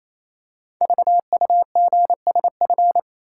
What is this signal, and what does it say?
Morse code, heard as a keyed tone, about 28 wpm, 700 Hz. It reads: VUGSF
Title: Code Groups - 5 Letters - 28wpm